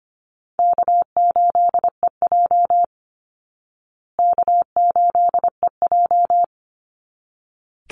Morse code, heard as a keyed tone, about 25 wpm, 700 Hz. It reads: K8EJ K8EJ
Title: Easy Call Signs Repeated 2x - 25wpm